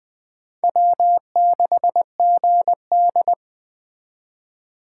Morse code, heard as a keyed tone, about 20 wpm, 700 Hz. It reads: W6GD
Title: Speed Racing - USA 1x2 Call Signs 17wpm